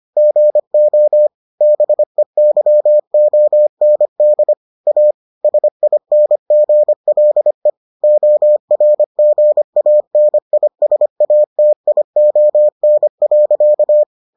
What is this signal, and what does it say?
Morse code, heard as a keyed tone, about 25 wpm, 600 Hz. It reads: GO BEYOND A SINGLE ORGANISATION.